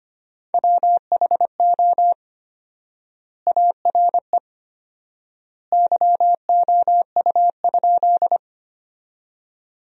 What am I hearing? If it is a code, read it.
Morse code, heard as a keyed tone, about 25 wpm, 700 Hz. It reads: WHO ARE YOU?
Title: 4x Word-Spacing - Sentences from Top 100 Words - 25wpm